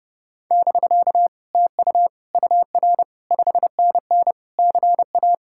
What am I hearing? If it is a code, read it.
Morse code, heard as a keyed tone, about 40 wpm, spaced as for 35 wpm, 700 Hz. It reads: <BK> TU UR 5NN CA
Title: Speed-Racing Common QSO Phrases - 30wpm